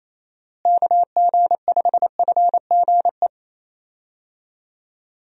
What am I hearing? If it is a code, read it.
Morse code, heard as a keyed tone, about 28 wpm, 700 Hz. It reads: KG5FGE
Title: USA Call Signs - 6 Characters - Repeated 2x - 28wpm